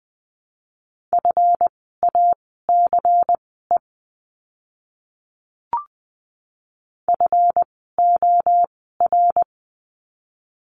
Morse code, heard as a keyed tone, about 20 wpm, spaced as for 15 wpm, 700 Hz. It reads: FACE FOR
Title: 139 Lesson 23f - Letter f - Words 20wpm